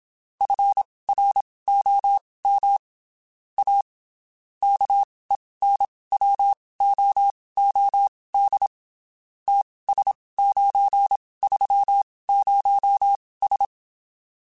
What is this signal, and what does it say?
Morse code, heard as a keyed tone, about 27 wpm, spaced as for 19 wpm, 800 Hz. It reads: FROM A KENWOOD TS930S
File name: (spaced) / QSOs Spaced-mp3s / WA7QOX 2 @25 sp